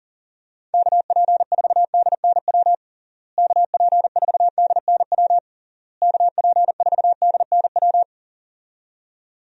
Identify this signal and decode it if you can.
Morse code, heard as a keyed tone, about 40 wpm, 700 Hz. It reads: KP4DNW KP4DNW KP4DNW
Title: USA Call Signs - 6 Characters - Repeated 3x - 40wpm